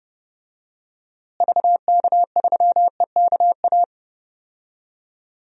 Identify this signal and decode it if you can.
Morse code, heard as a keyed tone, about 29 wpm, 700 Hz. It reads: VK3EKA
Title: DXCC-100 Call Signs - 6 Characters - Repeated 2x - 30wpm